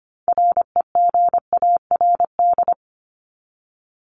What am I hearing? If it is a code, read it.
Morse code, heard as a keyed tone, about 25 wpm, 700 Hz. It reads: REGARD
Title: ED Suffix - Top 100 Words with Base Words 25wpm